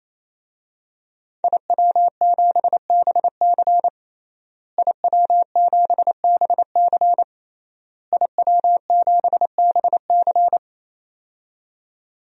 Morse code, heard as a keyed tone, about 27 wpm, 700 Hz. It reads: IW7BC IW7BC IW7BC
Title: DXCC-100 Call Signs - 5 Characters - Repeated 3x - 28wpm